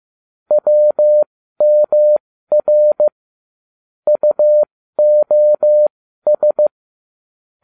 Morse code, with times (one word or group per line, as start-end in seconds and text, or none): 0.5-3.1 WMR
4.1-6.7 UOS